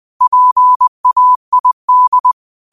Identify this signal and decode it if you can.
Morse code, heard as a keyed tone, about 20 wpm, 1000 Hz. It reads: PAID